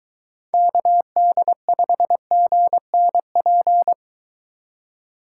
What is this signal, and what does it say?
Morse code, heard as a keyed tone, about 23 wpm, 700 Hz. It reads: KD5GNP